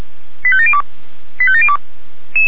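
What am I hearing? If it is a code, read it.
Five-tone selective call (ZVEI), heard as quick runs of five tones, tones of about 70 ms. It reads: 85702 85702
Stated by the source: ZVEI five-tone call received on SDR, from the Feuerwehr Orlamünde fire brigade audio server.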